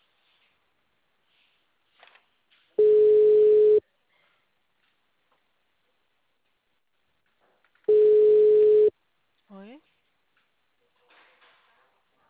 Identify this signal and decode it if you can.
Ringback tone (Europe, 425 Hz), with speech.
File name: out- -2034- - - .723